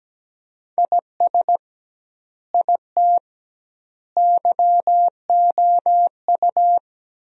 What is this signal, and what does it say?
Morse code, heard as a keyed tone, about 17 wpm, 700 Hz. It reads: IS IT YOU
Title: 2x Word-Spacing - Sentences from Top 200 Words - 17wpm